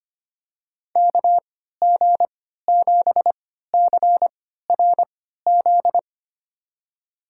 Morse code, heard as a keyed tone, about 25 wpm, spaced as for 14 wpm, 700 Hz. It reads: KG7CRZ
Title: USA Call Signs - 6 Characters - 3x Character Spacing - 25wpm